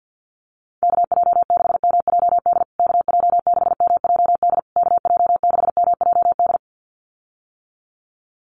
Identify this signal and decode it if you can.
Morse code, heard as a keyed tone, about 50 wpm, 700 Hz. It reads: KJ6MJD KJ6MJD KJ6MJD
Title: USA Call Signs Repeated 3x 50wpm